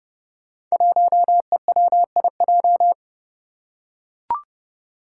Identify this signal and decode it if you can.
Morse code, heard as a keyed tone, about 30 wpm, 700 Hz. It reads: 1EWIJ
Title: Five Challenging Letter-Number Combinations 30wpm